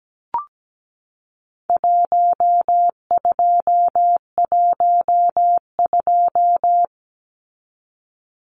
Morse code, heard as a keyed tone, about 17 wpm, 700 Hz. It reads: 1212
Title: Numbers - Contest Style - 4-Digit 17wpm